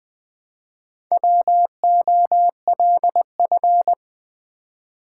Morse code, heard as a keyed tone, about 20 wpm, 700 Hz. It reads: WOLF
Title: Build A Word - 1 Letter At A Time - 4-Letter Words - 20wpm